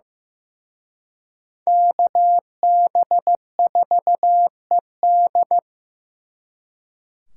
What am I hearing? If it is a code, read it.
Morse code, heard as a keyed tone, about 18 wpm, 700 Hz. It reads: KB4ED
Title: Speed Racing - USA 2x2 Call Signs 15wpm